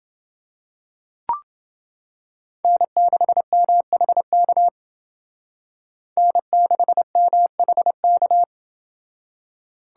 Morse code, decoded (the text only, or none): N6MHK N6MHK